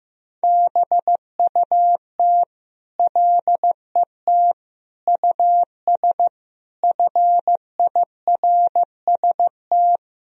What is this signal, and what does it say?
Morse code, heard as a keyed tone, about 15 wpm, 700 Hz. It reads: BUT LET US FIRST
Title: Sentences from Top 1000 Words - Review Part 4 - 15wpm